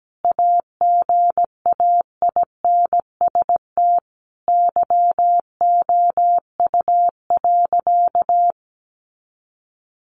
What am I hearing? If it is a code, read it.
Morse code, heard as a keyed tone, about 17 wpm, 700 Hz. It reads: AGAINST YOU.
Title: Sentences from Top 1200 Words 17wpm